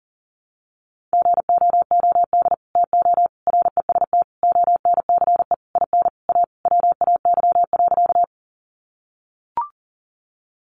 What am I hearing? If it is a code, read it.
Morse code, heard as a keyed tone, about 40 wpm, 700 Hz. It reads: GOOD TO REST ONCE IN A WAY.
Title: Sentences from Top 500 Words 40wpm